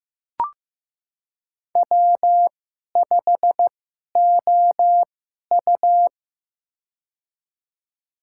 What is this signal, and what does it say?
Morse code, heard as a keyed tone, about 15 wpm, spaced as for 11 wpm, 700 Hz. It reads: W5OU
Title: USA Call Signs - 4 Characters - 2x Character Spacing - 15wpm